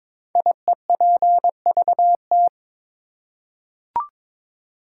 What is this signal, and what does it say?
Morse code, heard as a keyed tone, about 22 wpm, 700 Hz. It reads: IEPVT
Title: Code Groups - 5 Letters - 22wpm